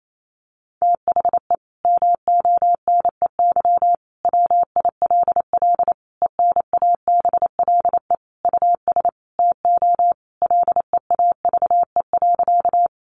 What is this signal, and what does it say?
Morse code, heard as a keyed tone, about 28 wpm, 700 Hz. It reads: THE MONEY WILL ENABLE US TO LEAVE.
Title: Sentences from Top 1500 Words 28wpm